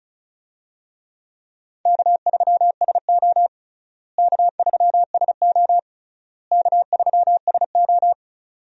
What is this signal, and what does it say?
Morse code, heard as a keyed tone, about 36 wpm, 700 Hz. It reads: K3SO K3SO K3SO
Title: USA Call Signs - 4 Characters - Repeated 3x - 35wpm